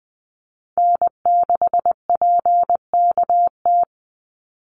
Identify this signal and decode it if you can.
Morse code, heard as a keyed tone, about 20 wpm, 700 Hz. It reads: N6PKT
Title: NA SOTA Chaser Callsigns 20wpm